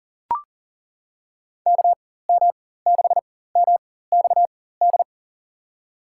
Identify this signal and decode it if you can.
Morse code, heard as a keyed tone, about 40 wpm, spaced as for 19 wpm, 700 Hz. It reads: KM6MXD
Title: USA Call Signs - 6 Characters - 4x Character Spacing - 40wpm